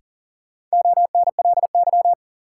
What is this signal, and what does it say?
Morse code, heard as a keyed tone, about 38 wpm, 700 Hz. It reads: ONLY
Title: Sentences from Top 200 Words - New Word Drill - 40wpm